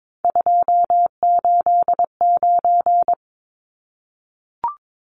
Morse code, heard as a keyed tone, about 22 wpm, 700 Hz. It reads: 289
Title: Numbers - 3-Digit 22wpm